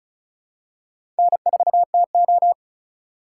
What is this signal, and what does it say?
Morse code, heard as a keyed tone, about 36 wpm, 700 Hz. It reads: N4TO
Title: USA Call Signs - 4x Character Spacing - 35wpm